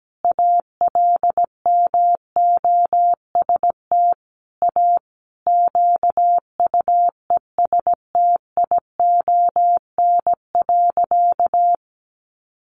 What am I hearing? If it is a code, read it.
Morse code, heard as a keyed tone, about 17 wpm, 700 Hz. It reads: ALMOST A QUESTION.
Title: Sentences from Top 1200 Words 17wpm